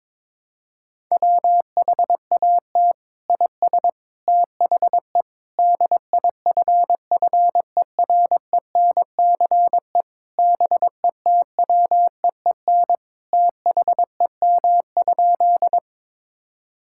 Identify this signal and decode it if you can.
Morse code, heard as a keyed tone, about 22 wpm, 700 Hz. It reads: WHAT IS THE DIFFERENCE BETWEEN THEM?
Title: Sentences from Top 1000 Words - Review Part 5 - 22wpm